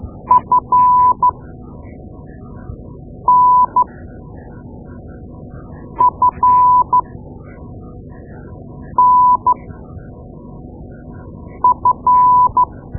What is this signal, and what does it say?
Morse code, heard as a keyed tone, about 12 wpm, spaced as for 4 wpm, 1000 Hz. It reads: FNFNF
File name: Morse Lessons / Morse Lessons / Tape-2A